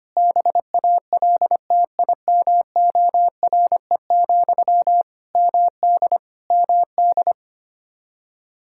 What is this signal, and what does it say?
Morse code, heard as a keyed tone, about 25 wpm, 700 Hz. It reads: BALTIMORE, MD MD
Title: Practice QSO - Exchange 1 - Beginning - 25wpm